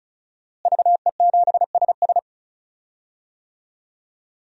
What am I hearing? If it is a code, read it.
Morse code, heard as a keyed tone, about 36 wpm, 700 Hz. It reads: VE7SS
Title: DXCC-100 Call Signs - 5 Characters - Repeated 3x - 35wpm